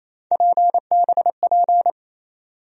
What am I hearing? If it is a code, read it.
Morse code, heard as a keyed tone, about 28 wpm, 700 Hz. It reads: PBP
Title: USA Call Sign Postfixes - 3 Letters 28wpm